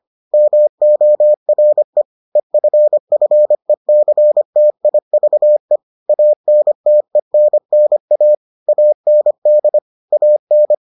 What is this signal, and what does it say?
Morse code, heard as a keyed tone, about 25 wpm, 600 Hz. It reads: MORE EFFECTIVE ANTENNA AND AN